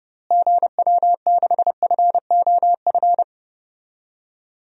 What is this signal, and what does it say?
Morse code, heard as a keyed tone, about 31 wpm, 700 Hz. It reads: GW6FOF
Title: DXCC-100 Call Signs - 6 Characters - Repeated 2x - 30wpm